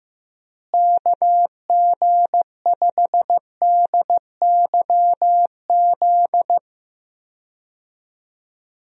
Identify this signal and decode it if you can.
Morse code, heard as a keyed tone, about 15 wpm, 700 Hz. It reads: KG5DYZ
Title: USA Call Signs - 6 Characters - Repeated 2x - 15wpm